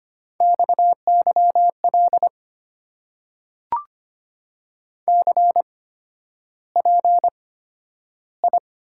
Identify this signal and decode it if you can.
Morse code, heard as a keyed tone, about 25 wpm, spaced as for 11 wpm, 700 Hz. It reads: XYL CPI
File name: Top 90 QSO Elements - 8x Character Spacing - 25wpm